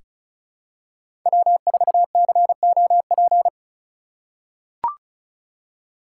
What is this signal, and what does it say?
Morse code, heard as a keyed tone, about 34 wpm, 700 Hz. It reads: W4COP
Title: USA Call Signs - 5 Characters - 2x Character Spacing - 35wpm